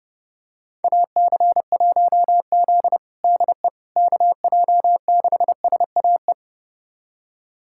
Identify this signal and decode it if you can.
Morse code, heard as a keyed tone, about 30 wpm, 700 Hz. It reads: AC1Z DE KJ6SAE
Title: USA Callsign DE Callsign 30wpm